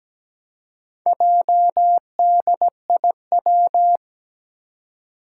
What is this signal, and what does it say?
Morse code, heard as a keyed tone, about 17 wpm, 700 Hz. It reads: JDIW
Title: Four Challenging Letter-Number Combinations 17wpm